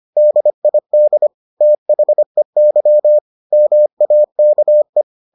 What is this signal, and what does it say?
Morse code, heard as a keyed tone, about 25 wpm, 600 Hz. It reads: DID THEY MAKE